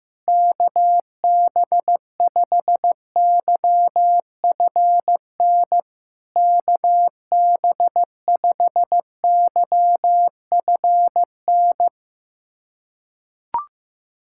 Morse code, heard as a keyed tone, about 17 wpm, 700 Hz. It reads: KB5YFN KB5YFN
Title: SOTA QSO Practice 15wpm